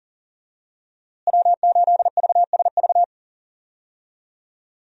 Morse code, heard as a keyed tone, about 40 wpm, 700 Hz. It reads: W8VSV
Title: USA Call Signs - 5 Characters - Repeated 2x - 40wpm